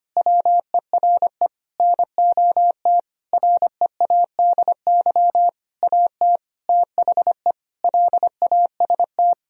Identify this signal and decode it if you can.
Morse code, heard as a keyed tone, about 25 wpm, 700 Hz. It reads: WERE NOT READY AT THE LAST
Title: Sentences from Top 1000 Words - Review Part 2 - 25wpm